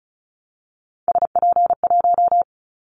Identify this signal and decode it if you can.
Morse code, heard as a keyed tone, about 34 wpm, 700 Hz. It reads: SP1
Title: Callsign Prefixes for Top 25 DXCC - 35wpm